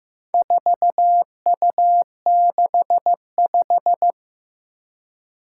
Morse code, heard as a keyed tone, about 15 wpm, 700 Hz. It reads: 4U65
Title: Four Challenging Letter-Number Combinations 15wpm